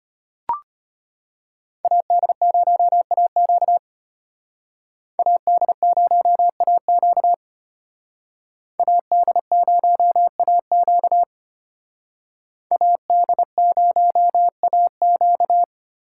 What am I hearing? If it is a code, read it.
Morse code, heard as a keyed tone, about 32 wpm, 700 Hz. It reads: AD0AQ AD0AQ AD0AQ AD0AQ